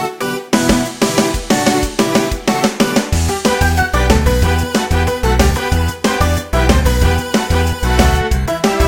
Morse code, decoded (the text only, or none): none